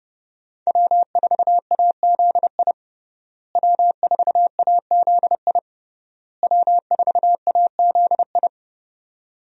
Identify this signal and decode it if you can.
Morse code, heard as a keyed tone, about 30 wpm, 700 Hz. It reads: W4AZI W4AZI W4AZI
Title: USA Call Signs - 5 Characters - Repeated 3x - 30wpm